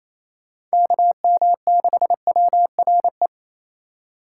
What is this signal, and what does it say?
Morse code, heard as a keyed tone, about 33 wpm, 700 Hz. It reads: KM6WRE
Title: Speed Racing - USA 2x3 Call Signs 28wpm